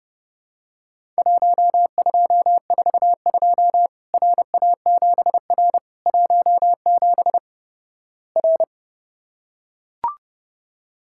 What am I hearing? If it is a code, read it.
Morse code, heard as a keyed tone, about 32 wpm, 700 Hz. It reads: 1242 RA7R 17 R
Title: Worked All Europe DX Contest - QTC Exchange - 30wpm